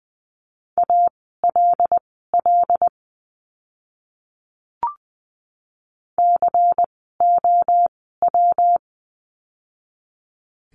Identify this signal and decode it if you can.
Morse code, heard as a keyed tone, about 20 wpm, spaced as for 14 wpm, 700 Hz. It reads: ALL COW
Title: 115 Lesson 19f - Letter w - Words 20wpm